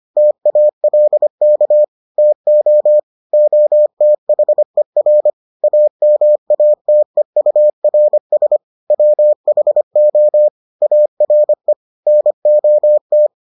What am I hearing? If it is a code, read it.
Morse code, heard as a keyed tone, about 25 wpm, 600 Hz. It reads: TALK TO OTHER AMATEURS WHO ARE NOT